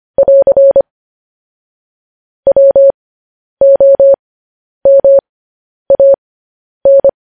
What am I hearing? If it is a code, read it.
Morse code, heard as a keyed tone, about 25 wpm, spaced as for 10 wpm, 550 Hz. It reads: <AR> WOMAN
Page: news headlines in morse code at 10 wpm